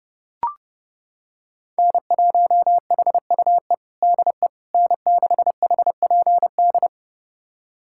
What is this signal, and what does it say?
Morse code, heard as a keyed tone, about 32 wpm, 700 Hz. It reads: N1HUE DE N6HPD
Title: USA Callsign DE Callsign 30wpm